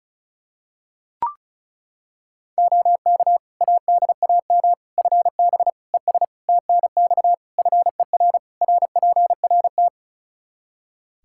Morse code, heard as a keyed tone, about 34 wpm, 700 Hz. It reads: OK ADAM FB ES TNX FER RPRT
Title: Practice QSO - Exchange 2 - Info - 35wpm